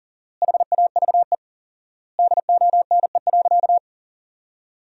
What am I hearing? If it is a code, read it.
Morse code, heard as a keyed tone, about 40 wpm, 700 Hz. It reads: HAVE DONE.
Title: 4x Word-Spacing - Sentences from Top 500 Words - 40wpm